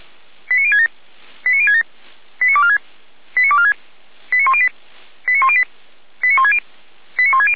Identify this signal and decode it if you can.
Five-tone selective call (ZVEI), heard as quick runs of five tones, tones of about 70 ms. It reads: 89067 89067 89246 89246 89108 89108 89160 89160